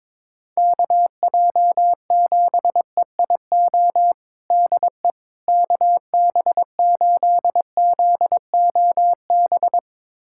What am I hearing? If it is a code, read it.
Morse code, heard as a keyed tone, about 22 wpm, 700 Hz. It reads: KJ7EIO DE KB8ZOB
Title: USA Callsign DE Callsign 22wpm